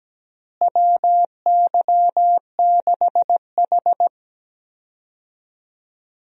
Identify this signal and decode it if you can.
Morse code, heard as a keyed tone, about 17 wpm, 700 Hz. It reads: WY6H
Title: USA Call Signs - 4 Characters - Repeated 3x - 17wpm